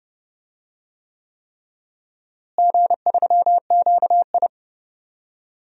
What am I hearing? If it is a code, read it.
Morse code, heard as a keyed tone, about 30 wpm, 700 Hz. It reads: G3QI
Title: DXCC-100 Call Signs - 4 Characters - Repeated 3x - 30wpm